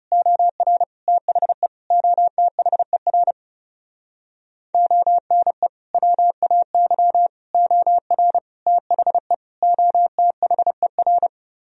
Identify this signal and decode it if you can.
Morse code, heard as a keyed tone, about 33 wpm, 700 Hz. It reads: OR THE OTHER ONE WAY OR THE OTHER
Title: Speed-Racing - Sets of 5 Words 30wpm